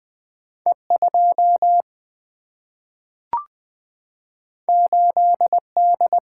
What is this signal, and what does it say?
Morse code, heard as a keyed tone, about 20 wpm, 700 Hz. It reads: E2 8D
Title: Single-Letter Single-Number 20wpm